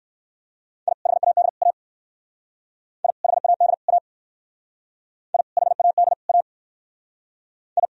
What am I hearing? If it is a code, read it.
Morse code, heard as a keyed tone, about 59 wpm, 700 Hz. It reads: I HAD A I HAD A I HAD A I